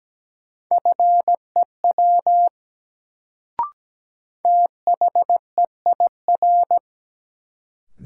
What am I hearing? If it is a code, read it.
Morse code, heard as a keyed tone, about 17 wpm, 700 Hz. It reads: FEW THEIR
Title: Sets of 2 Words - New Word Drill - 17wpm